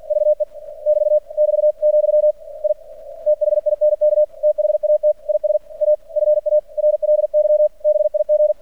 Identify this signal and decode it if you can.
Morse code, heard as a keyed tone, about 46 wpm, 600 Hz. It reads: VE <BT> <BT>, I THINK THAT IS A FARLY BIG
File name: archived / CRSnet